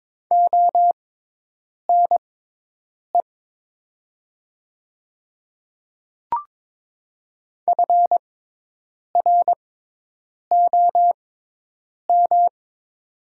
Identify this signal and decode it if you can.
Morse code, heard as a keyed tone, about 22 wpm, spaced as for 8 wpm, 700 Hz. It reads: ONE FROM
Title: 6x Character Spacing - Top 100 Words - 22wpm